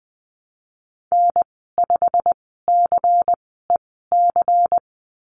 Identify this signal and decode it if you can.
Morse code, heard as a keyed tone, about 20 wpm, spaced as for 15 wpm, 700 Hz. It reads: N5CEC
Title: 147 Lesson 24g - Letter y - Call Signs 20wpm